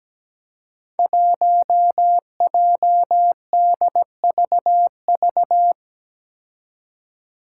Morse code, heard as a keyed tone, about 17 wpm, 700 Hz. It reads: 1JDVV
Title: Five Challenging Letter-Number Combinations 17wpm